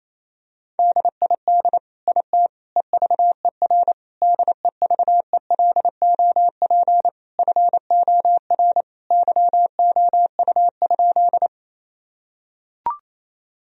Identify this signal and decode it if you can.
Morse code, heard as a keyed tone, about 28 wpm, 700 Hz. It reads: DID IT EVER DEVELOP FOR YOU?
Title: Sentences from Top 1000 Words - Review Part 2 - 28wpm